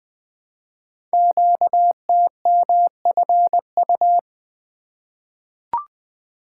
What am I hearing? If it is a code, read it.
Morse code, heard as a keyed tone, about 20 wpm, 700 Hz. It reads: QTMFU
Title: Code Groups - 5 Letters - 20wpm